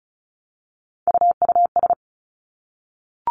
Morse code, heard as a keyed tone, about 36 wpm, 700 Hz. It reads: UUS
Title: Three Challenging Letter-Number Combinations 35wpm